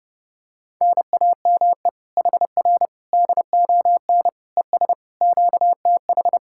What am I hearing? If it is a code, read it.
Morse code, heard as a keyed tone, about 29 wpm, 700 Hz. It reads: NAME HR DON ES QTH